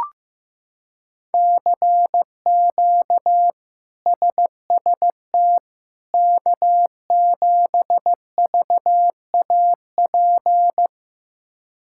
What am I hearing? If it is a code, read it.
Morse code, heard as a keyed tone, about 15 wpm, 700 Hz. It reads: CQ SST K7VAP
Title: K1USN SST QSO Practice - 15wpm